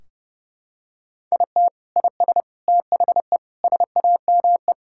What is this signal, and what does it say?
Morse code, heard as a keyed tone, about 30 wpm, 700 Hz. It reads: IT IS THE SAME
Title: Sentences from Top 1600 Words 30wpm